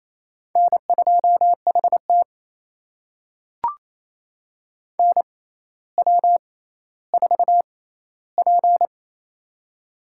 Morse code, heard as a keyed tone, about 27 wpm, spaced as for 14 wpm, 700 Hz. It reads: N2HT NW4P
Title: USA Call Signs - 4 Characters - 6x Character Spacing - 28wpm